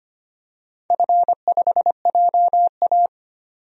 Morse code, heard as a keyed tone, about 25 wpm, 700 Hz. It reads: F5JA
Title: DXCC-100 Call Signs - 4 Characters - 25wpm